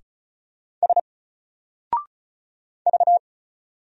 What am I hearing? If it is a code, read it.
Morse code, heard as a keyed tone, about 36 wpm, 700 Hz. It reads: S V
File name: Single-Letter-Number-35wpm